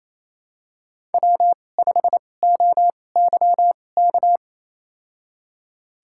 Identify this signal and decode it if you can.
Morse code, heard as a keyed tone, about 28 wpm, spaced as for 20 wpm, 700 Hz. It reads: W5OYK
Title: USA Call Signs - 5 Characters - 2x Character Spacing - 28wpm